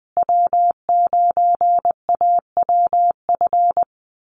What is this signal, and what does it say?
Morse code, heard as a keyed tone, about 20 wpm, 700 Hz. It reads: W9AWF
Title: USA Call Signs Repeated 2x 20wpm